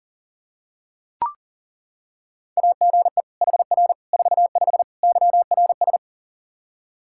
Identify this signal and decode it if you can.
Morse code, heard as a keyed tone, about 40 wpm, 700 Hz. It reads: AGE HR 45 YRS
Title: K1USN SST QSO Practice 40wpm